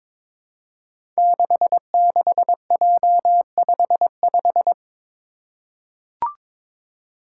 Morse code, heard as a keyed tone, about 22 wpm, 700 Hz. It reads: 66J55
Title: Five Challenging Letter-Number Combinations 22wpm